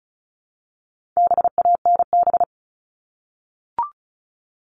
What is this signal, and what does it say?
Morse code, heard as a keyed tone, about 34 wpm, 700 Hz. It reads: BANB
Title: All Cut Numbers - 4-Digit 35wpm